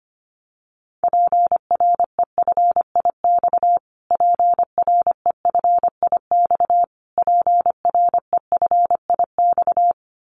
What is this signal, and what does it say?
Morse code, heard as a keyed tone, about 25 wpm, 700 Hz. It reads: PREFIX PREFIX PREFIX
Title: Callsign Prefixes for Top 25 DXCC - Repeated 3x - 25wpm